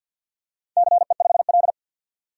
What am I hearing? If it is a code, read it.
Morse code, heard as a keyed tone, about 53 wpm, 700 Hz. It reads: CE5L
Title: DXCC-100 Call Signs - 4 Characters - 50wpm